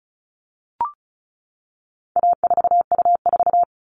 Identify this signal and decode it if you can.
Morse code, heard as a keyed tone, about 34 wpm, 700 Hz. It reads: A4U4
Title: All Cut Numbers - 4-Digit 35wpm